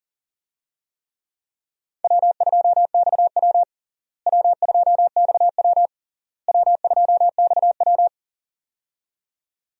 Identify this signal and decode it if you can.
Morse code, heard as a keyed tone, about 40 wpm, 700 Hz. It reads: W2XW W2XW W2XW
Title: USA Call Signs - 4 Characters - Repeated 3x - 40wpm